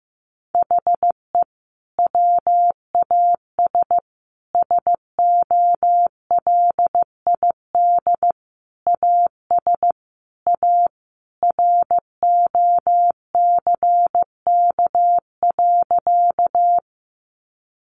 Morse code, heard as a keyed tone, about 15 wpm, 700 Hz. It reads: HE WAS SOLID AS A ROCK.
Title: Sentences from Top 1500 Words 15wpm